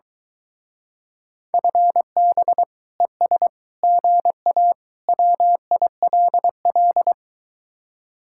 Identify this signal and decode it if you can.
Morse code, heard as a keyed tone, about 23 wpm, 700 Hz. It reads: FB ES GA WILL